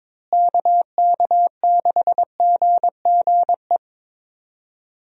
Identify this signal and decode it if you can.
Morse code, heard as a keyed tone, about 22 wpm, 700 Hz. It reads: KK6GGE
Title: USA Call Signs - 6 Characters - Repeated 2x - 22wpm